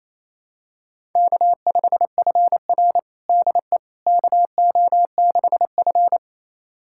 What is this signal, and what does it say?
Morse code, heard as a keyed tone, about 28 wpm, 700 Hz. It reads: K5FR DE KO6F
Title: USA Callsign DE Callsign - 4 Characters - 28wpm